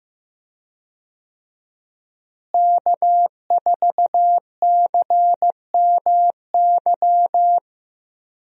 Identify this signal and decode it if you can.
Morse code, heard as a keyed tone, about 15 wpm, 700 Hz. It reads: K4CMY
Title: USA Call Signs - 5 Characters - Repeated 2x - 15wpm